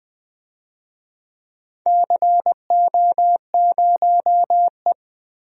Morse code, heard as a keyed tone, about 20 wpm, 700 Hz. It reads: CO0E
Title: DXCC-100 Call Signs - 4 Characters - Repeated 2x - 20wpm